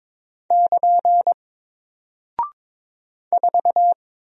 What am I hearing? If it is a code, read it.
Morse code, heard as a keyed tone, about 22 wpm, 700 Hz. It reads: <KN> 4
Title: Single Character (Letter:Number:Punctuation:Prosign) Alt 22wpm